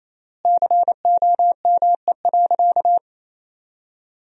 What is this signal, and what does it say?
Morse code, heard as a keyed tone, about 28 wpm, 700 Hz. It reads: COME.
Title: 3x Word-Spacing - Sentences from Top 100 Words - 28wpm